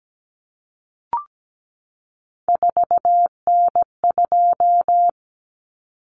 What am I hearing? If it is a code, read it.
Morse code, heard as a keyed tone, about 17 wpm, 700 Hz. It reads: 4N2
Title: Common Cut Numbers - 3-Digit 17wpm